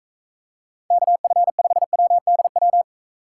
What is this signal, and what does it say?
Morse code, heard as a keyed tone, about 42 wpm, 700 Hz. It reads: KF5WDW